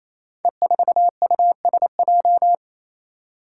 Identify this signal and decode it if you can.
Morse code, heard as a keyed tone, about 28 wpm, 700 Hz. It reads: E4USJ
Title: Five Challenging Letter-Number Combinations 28wpm